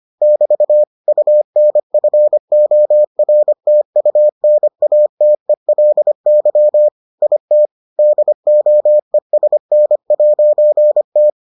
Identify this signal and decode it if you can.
Morse code, heard as a keyed tone, about 25 wpm, 600 Hz. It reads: <BT> UNFORTUNATELY IT DOESN'T